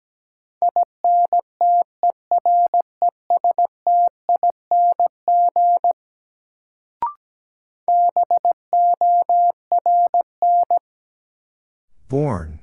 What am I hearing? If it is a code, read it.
Morse code, heard as a keyed tone, about 17 wpm, 700 Hz. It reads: INTERESTING BORN
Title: Sentences from Top 1000 Words - New Word Drill - 17wpm